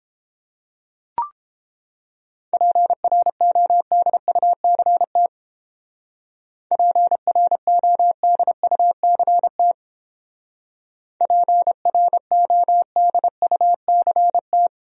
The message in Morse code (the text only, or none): PRODUCT PRODUCT PRODUCT